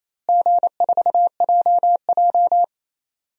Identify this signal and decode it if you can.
Morse code, heard as a keyed tone, about 28 wpm, 700 Hz. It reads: G4JJ
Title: DXCC-100 Call Signs - 4 Characters - Repeated 3x - 28wpm